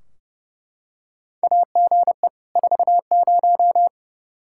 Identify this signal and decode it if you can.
Morse code, heard as a keyed tone, about 30 wpm, 700 Hz. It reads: AGE 40
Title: Practice QSO - Exchange 3 - More Info - 30wpm